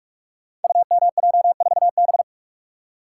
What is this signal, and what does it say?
Morse code, heard as a keyed tone, about 44 wpm, 700 Hz. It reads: UMJ4B
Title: Code Groups - 5 Letter:Numbers - 45wpm